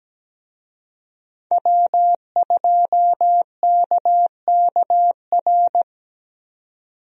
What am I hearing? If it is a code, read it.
Morse code, heard as a keyed tone, about 17 wpm, 700 Hz. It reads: W2KKR
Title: USA Call Signs - 5 Characters - Repeated 2x - 17wpm